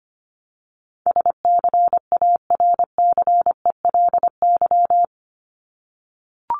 Morse code, heard as a keyed tone, about 25 wpm, 700 Hz. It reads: SCARCELY
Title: LY Suffix - Top 100 Words with Base Words 25wpm